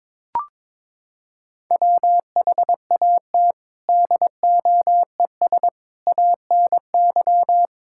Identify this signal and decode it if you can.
Morse code, heard as a keyed tone, about 22 wpm, 700 Hz. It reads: WHAT DOES ANY
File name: Sentences with 9-Word Phrases - 22wpm